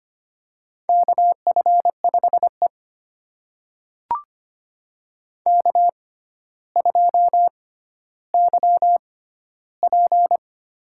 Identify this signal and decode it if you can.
Morse code, heard as a keyed tone, about 25 wpm, spaced as for 13 wpm, 700 Hz. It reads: KF5E K2YP
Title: USA Call Signs - 4 Characters - 6x Character Spacing - 25wpm